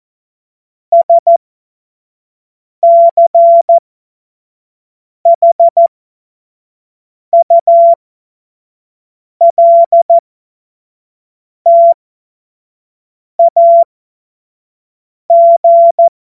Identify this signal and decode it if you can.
Morse code, heard as a keyed tone, about 14 wpm, spaced as for 5 wpm, 680 Hz. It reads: SCHULTAG
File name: Morsekurs / Witze / Witz 14. Februar 2023 25 F70